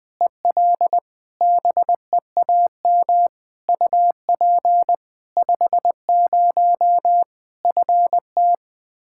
Morse code, heard as a keyed tone, about 20 wpm, 700 Hz. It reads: EL BEAM UP 50 FT
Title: Practice QSO - Exchange 2 - Info - 20wpm